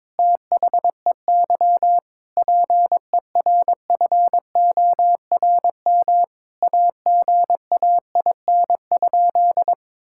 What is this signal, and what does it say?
Morse code, heard as a keyed tone, about 22 wpm, 700 Hz. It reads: THEY PERFORM AGAIN?
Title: Sentences from Top 1000 Words - Review Part 5 - 22wpm